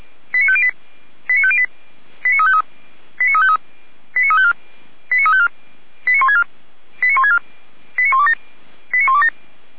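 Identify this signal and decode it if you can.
Five-tone selective call (ZVEI), heard as quick runs of five tones, tones of about 70 ms. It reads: 89508 89508 89353 89353 89354 89354 89164 89164 89127 89127
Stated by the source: ZVEI five-tone call received by SDR.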